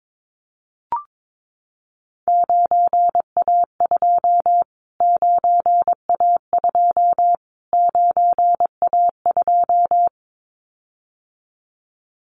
Morse code, heard as a keyed tone, about 22 wpm, 700 Hz. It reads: 9A2 9A2 9A2
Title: Callsign Prefixes for Top 25 DXCC - Repeated 3x - 22wpm